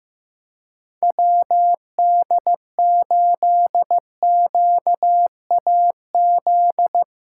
Morse code, heard as a keyed tone, about 15 wpm, 700 Hz. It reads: WD8QAZ
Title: USA Callsign DE Callsign - 6 Characters - 15wpm